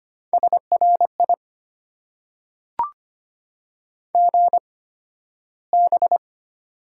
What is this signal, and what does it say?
Morse code, heard as a keyed tone, about 25 wpm, spaced as for 13 wpm, 690 Hz. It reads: SRI GB
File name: Top 90 QSO Elements - 8x Character Spacing - 25wpm